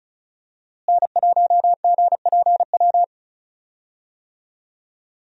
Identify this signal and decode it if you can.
Morse code, heard as a keyed tone, about 36 wpm, 700 Hz. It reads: N1GPW
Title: USA Call Signs - 5 Characters - Repeated 2x - 35wpm